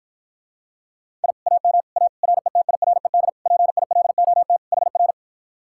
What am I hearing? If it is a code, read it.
Morse code, heard as a keyed tone, about 54 wpm, 700 Hz. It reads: I AM A RETIRED PILOT HR